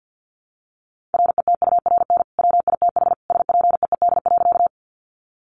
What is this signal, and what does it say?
Morse code, heard as a keyed tone, about 50 wpm, 700 Hz. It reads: RETURN WITH SPEED.
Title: Sentences from Top 800 Words 50wpm